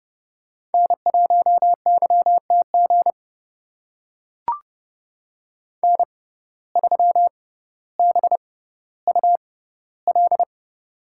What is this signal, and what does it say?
Morse code, heard as a keyed tone, about 30 wpm, spaced as for 15 wpm, 700 Hz. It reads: N1YTG N3BUL
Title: USA Call Signs - 5 Characters - 6x Character Spacing - 30wpm